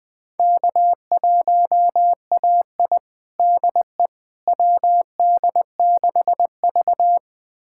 Morse code, heard as a keyed tone, about 20 wpm, 700 Hz. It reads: K1AI DE WD6V
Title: USA Callsign DE Callsign - 4 Characters - 20wpm